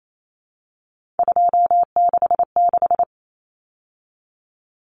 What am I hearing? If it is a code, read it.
Morse code, heard as a keyed tone, about 28 wpm, 700 Hz. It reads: 266
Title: Common Cut Numbers - 3-Digit 28wpm